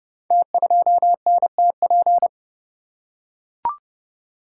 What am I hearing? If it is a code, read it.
Morse code, heard as a keyed tone, about 30 wpm, 700 Hz. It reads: T2NTP
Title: Code Groups - 5 Letter:Numbers - 30wpm